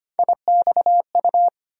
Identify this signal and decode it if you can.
Morse code, heard as a keyed tone, about 25 wpm, 700 Hz. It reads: IXU
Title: USA Call Sign Postfixes - 3 Letters 25wpm